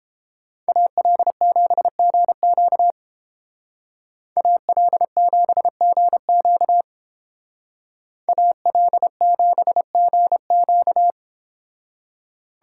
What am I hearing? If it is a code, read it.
Morse code, heard as a keyed tone, about 29 wpm, 700 Hz. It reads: AL7GQ AL7GQ AL7GQ